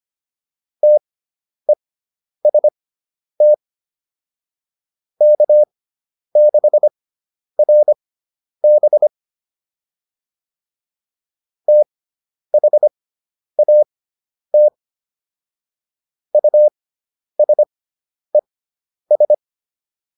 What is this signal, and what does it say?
Morse code, heard as a keyed tone, about 25 wpm, spaced as for 10 wpm, 600 Hz. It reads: TEST K6RB THAT USES